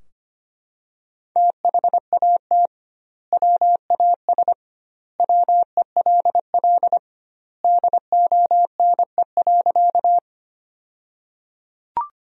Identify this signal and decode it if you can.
Morse code, heard as a keyed tone, about 25 wpm, 700 Hz. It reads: THAT WAS WELL DONE.
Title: 2x Word-Spacing - Sentences from Top 500 Words - 25wpm